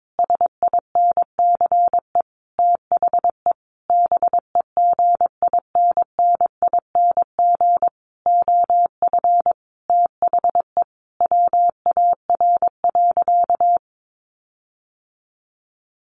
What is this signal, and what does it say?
Morse code, heard as a keyed tone, about 22 wpm, 700 Hz. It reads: SINCE THE BEGINNING OF THE WAR.
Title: Sentences from Top 1400 Words 22wpm